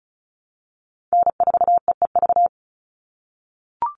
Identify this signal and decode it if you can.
Morse code, heard as a keyed tone, about 34 wpm, 700 Hz. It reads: N4EEV
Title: Call Signs - 35wpm